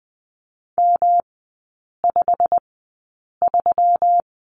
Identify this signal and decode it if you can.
Morse code, heard as a keyed tone, about 20 wpm, spaced as for 8 wpm, 690 Hz. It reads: M53
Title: 172 Lesson 28e - Number 9 - Three Characters 20wpm